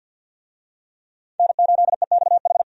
Morse code, heard as a keyed tone, about 52 wpm, 700 Hz. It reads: N8EXH
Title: USA Call Signs - 5 Characters - 50wpm